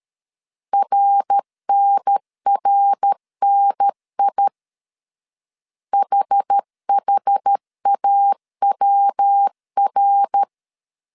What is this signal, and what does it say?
Morse code, heard as a keyed tone, about 13 wpm, 800 Hz. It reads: RNRNI HHAWR
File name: CW / DET / 1 3 1500C S B 12wpm 2828